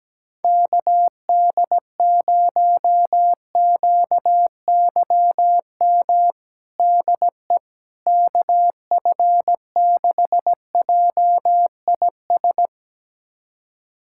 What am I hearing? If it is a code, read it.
Morse code, heard as a keyed tone, about 17 wpm, 700 Hz. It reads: KD0QYM DE KF6JIS
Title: USA Callsign DE Callsign - 6 Characters - 17wpm